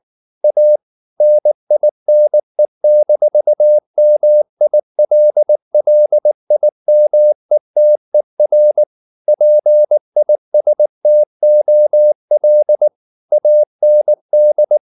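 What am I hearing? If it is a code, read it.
Morse code, heard as a keyed tone, about 19 wpm, 600 Hz. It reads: A NINE-MILLIMETER PISTOL AND
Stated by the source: Morse code at 19 wpm from CW med Gnister, reading Ken Follett's novel Never in English.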